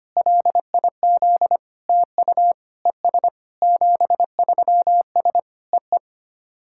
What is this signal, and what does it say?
Morse code, heard as a keyed tone, about 25 wpm, 690 Hz. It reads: LIZ TU ES 73S EE
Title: SOTA QSO Practice 25wpm